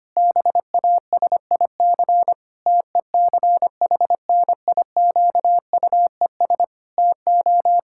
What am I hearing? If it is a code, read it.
Morse code, heard as a keyed tone, about 25 wpm, 700 Hz. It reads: BASIC TECHNIQUES TO
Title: Pangram Challenge v2 - 25wpm